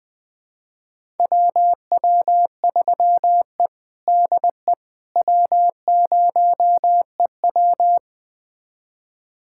Morse code, heard as a keyed tone, about 20 wpm, 700 Hz. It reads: WW3E DE W0EW
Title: USA Callsign DE Callsign - 4 Characters - 20wpm